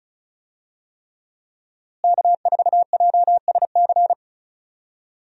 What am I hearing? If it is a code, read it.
Morse code, heard as a keyed tone, about 36 wpm, 700 Hz. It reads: K4JSC
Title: USA Call Signs - 5 Characters - Repeated 3x - 35wpm